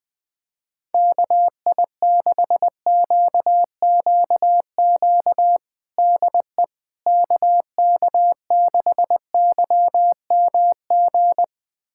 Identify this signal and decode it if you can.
Morse code, heard as a keyed tone, about 21 wpm, 700 Hz. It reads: KI6QQQ DE KK6YMG
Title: USA Callsign DE Callsign 20wpm